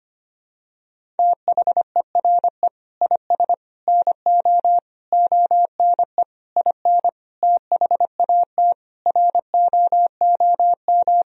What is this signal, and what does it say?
Morse code, heard as a keyed tone, about 25 wpm, 700 Hz. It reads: THERE IS NO ONE IN THAT ROOM
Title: Sentences from Top 500 Words and Abbreviations 25wpm